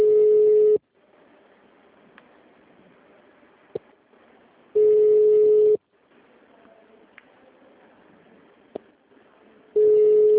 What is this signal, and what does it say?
Ringback tone (Europe, 425 Hz).
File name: out- -2001- - - .1454